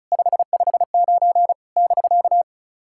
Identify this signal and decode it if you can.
Morse code, heard as a keyed tone, about 34 wpm, 700 Hz. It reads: 559 <BK>
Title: SOTA QSO Practice 35wpm